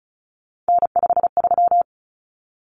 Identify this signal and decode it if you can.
Morse code, heard as a keyed tone, about 34 wpm, 700 Hz. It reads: N53
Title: Common Cut Numbers - 3-Digit 35wpm